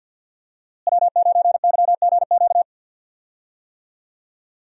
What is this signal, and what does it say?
Morse code, heard as a keyed tone, about 48 wpm, 700 Hz. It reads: W9YGQ
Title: USA Call Signs - 5 Characters - Repeated 3x - 50wpm